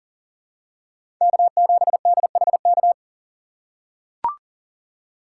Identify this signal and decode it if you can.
Morse code, heard as a keyed tone, about 37 wpm, 700 Hz. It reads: K7DHK
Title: USA Call Signs - 5 Characters - 2x Character Spacing - 40wpm